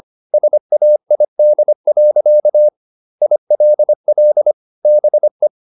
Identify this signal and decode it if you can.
Morse code, heard as a keyed tone, about 25 wpm, 600 Hz. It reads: SAID. ILL BE
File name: Never 0025wpm